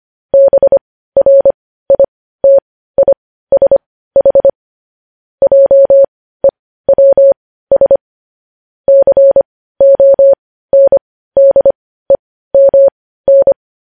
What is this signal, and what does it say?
Morse code, heard as a keyed tone, about 25 wpm, spaced as for 15 wpm, 550 Hz. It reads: BRITISH JEWS CONDEMN
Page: news-headlines-in-morse-code-at-15-wpm